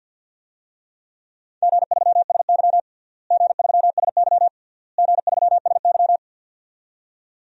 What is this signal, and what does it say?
Morse code, heard as a keyed tone, about 51 wpm, 700 Hz. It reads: G3SY G3SY G3SY
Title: DXCC-100 Call Signs - 4 Characters - Repeated 3x - 50wpm